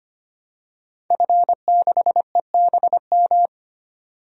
Morse code, heard as a keyed tone, about 25 wpm, 700 Hz. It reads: F6EBM
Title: DXCC-100 Call Signs - 5 Characters - 25wpm